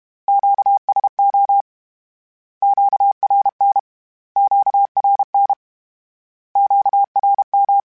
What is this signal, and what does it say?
Morse code, heard as a keyed tone, about 32 wpm, 800 Hz. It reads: QSO QRN QRN QRM
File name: Q Calls / Q Calls-mp3s / Q Calls Twice @30